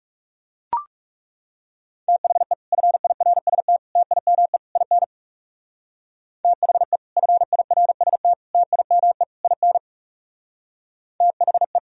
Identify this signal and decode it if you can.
Morse code, heard as a keyed tone, about 40 wpm, 700 Hz. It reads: THE FIRST TIME IN THE FIRST TIME IN THE